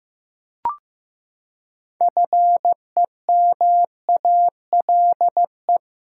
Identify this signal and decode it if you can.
Morse code, heard as a keyed tone, about 15 wpm, 700 Hz. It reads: FEMALE
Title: Six Letter Words - 15wpm